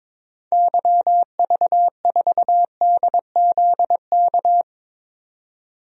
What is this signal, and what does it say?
Morse code, heard as a keyed tone, about 22 wpm, 700 Hz. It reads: YV4DZK
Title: DXCC-100 Call Signs - 6 Characters - 22wpm